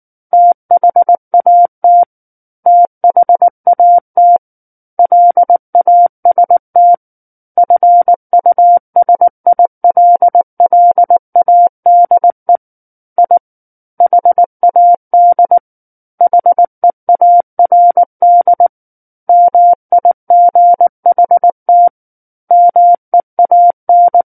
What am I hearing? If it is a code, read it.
Morse code, heard as a keyed tone, about 19 wpm, 700 Hz. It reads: THAT THAT LAST FUSILLADE I HAD HEARD MIGHT MEAN
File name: Book One / 10-Chapter 10 - 19 WPM